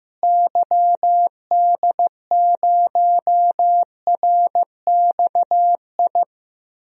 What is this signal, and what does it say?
Morse code, heard as a keyed tone, about 15 wpm, 700 Hz. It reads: YD0RXI
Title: DXCC-100 Call Signs - 6 Characters - Repeated 2x - 15wpm